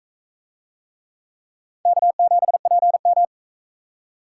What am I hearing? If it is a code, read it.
Morse code, heard as a keyed tone, about 38 wpm, 700 Hz. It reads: K7PM